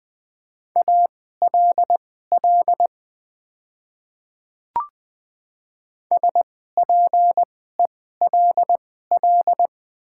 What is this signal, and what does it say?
Morse code, heard as a keyed tone, about 20 wpm, spaced as for 15 wpm, 700 Hz. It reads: ALL SPELL
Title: 2x Character Spacing - Top 200 Words - 20wpm